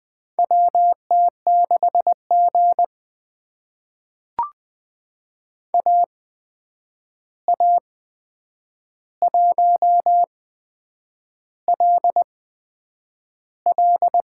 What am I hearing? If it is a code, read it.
Morse code, heard as a keyed tone, about 20 wpm, spaced as for 8 wpm, 700 Hz. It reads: WT6G AA1LL
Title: USA Call Signs - 8x Character Spacing - 20wpm